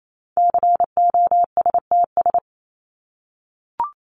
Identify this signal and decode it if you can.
Morse code, heard as a keyed tone, about 28 wpm, 700 Hz. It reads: COSTS
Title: ES:S Suffix - Top 100 Words with Base Words 28wpm